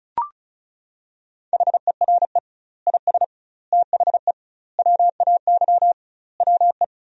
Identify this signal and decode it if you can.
Morse code, heard as a keyed tone, about 34 wpm, 700 Hz. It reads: HERE IS THE WAY WE
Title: 2x Word-Spacing - Sentences from Top 200 Words - 35wpm